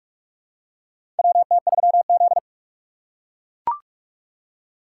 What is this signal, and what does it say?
Morse code, heard as a keyed tone, about 43 wpm, 700 Hz. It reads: WT3Z
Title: USA Call Signs - 4 Characters - 6x Character Spacing - 45wpm